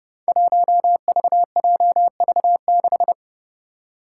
Morse code, heard as a keyed tone, about 30 wpm, 700 Hz. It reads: 1VJV6
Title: Five Challenging Letter-Number Combinations 30wpm